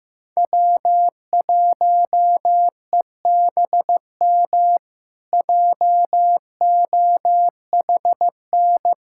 Morse code, heard as a keyed tone, about 15 wpm, 700 Hz. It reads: W1EBM JOHN
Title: K1USN SST QSO Practice - 15wpm